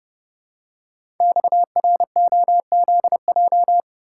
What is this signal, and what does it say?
Morse code, heard as a keyed tone, about 30 wpm, 700 Hz. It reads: XROZJ
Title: Code Groups - 5 Letters - 30wpm